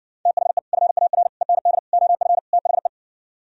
Morse code, heard as a keyed tone, about 59 wpm, 700 Hz. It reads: THE FAR END OF THE